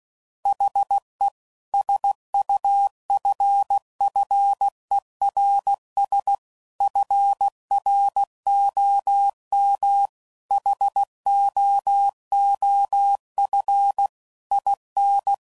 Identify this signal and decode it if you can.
Morse code, heard as a keyed tone, about 16 wpm, 800 Hz. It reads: HE SUFFERS FROM HOOF IN